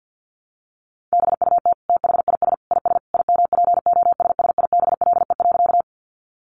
Morse code, heard as a keyed tone, about 50 wpm, 700 Hz. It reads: BUT THIS IS IMPOSSIBLE.
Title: Sentences from Top 1100 Words 50wpm